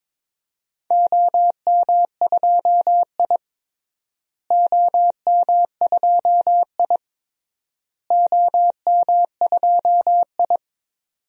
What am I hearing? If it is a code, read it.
Morse code, heard as a keyed tone, about 22 wpm, 700 Hz. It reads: OM2I OM2I OM2I
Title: DXCC-100 Call Signs - 4 Characters - Repeated 3x - 22wpm